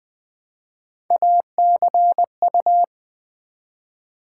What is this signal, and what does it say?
Morse code, heard as a keyed tone, about 20 wpm, 700 Hz. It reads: ACU
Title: Build A Word - 1 Letter At A Time - 5-Letter Words - 20wpm